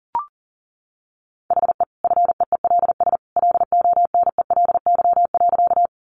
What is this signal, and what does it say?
Morse code, heard as a keyed tone, about 39 wpm, 700 Hz. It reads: HE FEELS LONELY.
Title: Easy Phrases 40wpm